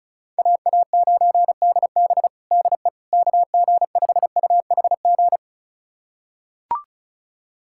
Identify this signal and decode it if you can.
Morse code, heard as a keyed tone, about 34 wpm, 700 Hz. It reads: AA9DB DE KG5UHG
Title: USA Callsign DE Callsign 35wpm